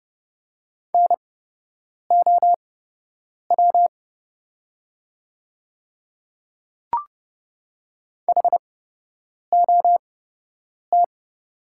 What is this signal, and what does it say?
Morse code, heard as a keyed tone, about 30 wpm, spaced as for 8 wpm, 700 Hz. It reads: NOW HOT
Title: 8x Character Spacing - Top 500 Words - 30wpm